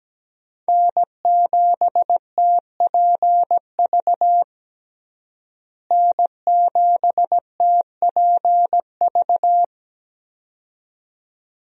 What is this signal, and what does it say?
Morse code, heard as a keyed tone, about 17 wpm, 700 Hz. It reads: N7TPV N7TPV
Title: USA Call Signs - 5 Characters - Repeated 2x - 17wpm